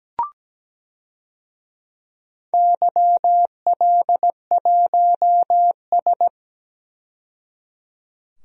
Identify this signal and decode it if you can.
Morse code, heard as a keyed tone, about 17 wpm, 700 Hz. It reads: YL1S
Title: DXCC-100 Call Signs - 4 Characters - 17wpm